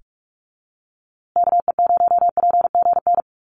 Morse code, heard as a keyed tone, about 44 wpm, 700 Hz. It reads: KE0PGN
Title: USA Call Signs Repeated 3x 45wpm